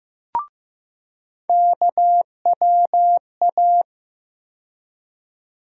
Morse code, heard as a keyed tone, about 15 wpm, 700 Hz. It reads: KWA
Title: USA Call Sign Postfixes - 3 Letters 15wpm